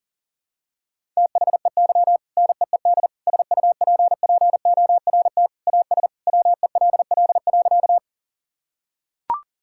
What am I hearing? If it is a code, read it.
Morse code, heard as a keyed tone, about 40 wpm, 700 Hz. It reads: THEY NEED SUPPORT AS WELL.
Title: Sentences from Top 1000 Words - Review Part 5 - 40wpm